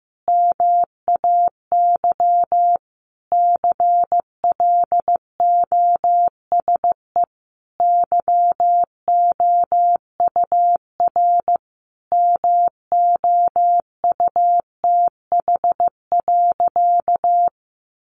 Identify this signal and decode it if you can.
Morse code, heard as a keyed tone, about 15 wpm, 700 Hz. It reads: MAY CLOSE YOUR MOUTH.
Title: Sentences from Top 1500 Words 15wpm